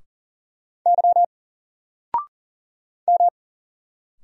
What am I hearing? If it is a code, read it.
Morse code, heard as a keyed tone, about 40 wpm, 700 Hz. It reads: Y M
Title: Single Letter-Number Alt 40wpm